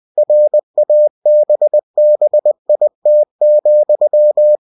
Morse code, heard as a keyed tone, about 20 wpm, 600 Hz. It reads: RABBIT,
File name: cw-books / winnie-the-pooh / winnie-20wpm-page-0019